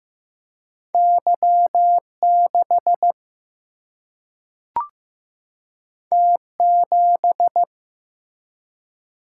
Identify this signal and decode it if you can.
Morse code, heard as a keyed tone, about 15 wpm, 700 Hz. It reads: Y6 T7
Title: Single-Letter Single-Number 15wpm